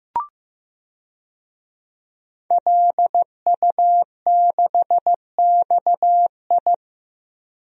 Morse code, heard as a keyed tone, about 15 wpm, 700 Hz. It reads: LU6XI
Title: DXCC-100 Call Signs - 5 Characters - 15wpm